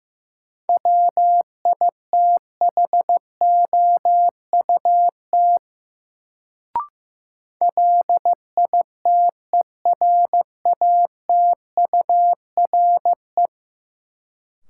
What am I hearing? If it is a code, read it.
Morse code, heard as a keyed tone, about 15 wpm, 700 Hz. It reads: WITHOUT LITERATURE
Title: Sentences from Top 600 Words - New Word Drill - 15wpm